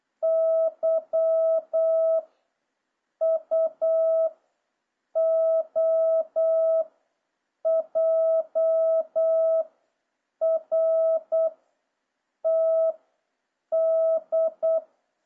Morse code, heard as a keyed tone, about 8 wpm, 650 Hz. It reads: Y U O J R T D